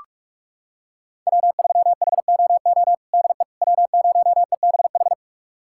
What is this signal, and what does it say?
Morse code, heard as a keyed tone, about 46 wpm, 700 Hz. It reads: W3HOO DE W0EBH
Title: USA Callsign DE Callsign - 5 Characters - 45wpm